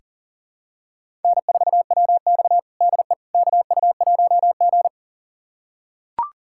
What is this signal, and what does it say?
Morse code, heard as a keyed tone, about 40 wpm, 700 Hz. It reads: N4WX DE KU1G
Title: USA Callsign DE Callsign - 4 Characters - 40wpm